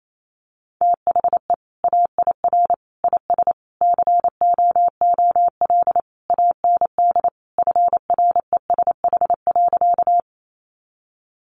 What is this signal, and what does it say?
Morse code, heard as a keyed tone, about 28 wpm, 700 Hz. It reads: THE AIR IS COOL AND FRESH.
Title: Sentences from Top 1200 Words 28wpm